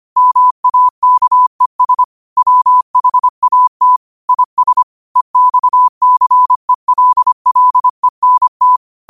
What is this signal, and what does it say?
Morse code, heard as a keyed tone, about 25 wpm, 1000 Hz. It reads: MAKES WHAT IS EXCELLENT